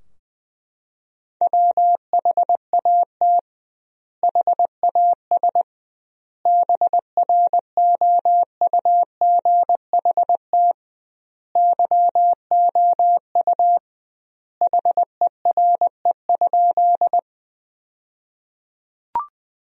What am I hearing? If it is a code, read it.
Morse code, heard as a keyed tone, about 20 wpm, 700 Hz. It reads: WHAT HAS BROUGHT YOU HERE?
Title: 2x Word-Spacing - Sentences from Top 500 Words - 20wpm